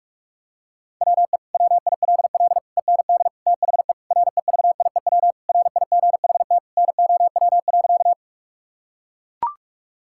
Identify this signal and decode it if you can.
Morse code, heard as a keyed tone, about 42 wpm, 700 Hz. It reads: WE WILL END THE REVIEW RIGHT NOW.
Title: Sentences from Top 1000 Words - Review Part 6 - 45wpm